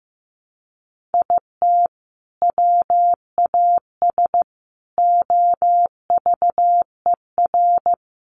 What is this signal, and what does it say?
Morse code, heard as a keyed tone, about 14 wpm, 700 Hz. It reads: IT WAS OVER
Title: Sentences from Top 600 Words 15wpm